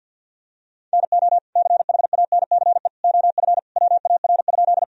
Morse code, heard as a keyed tone, about 50 wpm, 700 Hz. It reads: NO CHANCE OF WAR?
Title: Sentences from Top 1000 Words - Review Part 2 - 50wpm